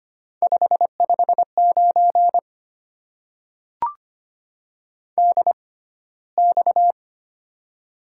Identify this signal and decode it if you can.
Morse code, heard as a keyed tone, about 25 wpm, spaced as for 15 wpm, 700 Hz. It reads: 559 DX
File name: Top 90 QSO Elements - 6x Character Spacing - 25wpm